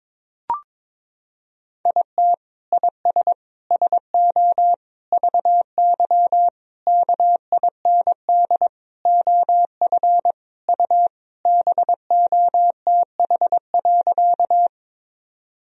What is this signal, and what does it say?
Morse code, heard as a keyed tone, about 22 wpm, 700 Hz. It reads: IT IS SO VY KIND OF U BOTH.
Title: Sentences from Top 300 Words and Abbreviations 22wpm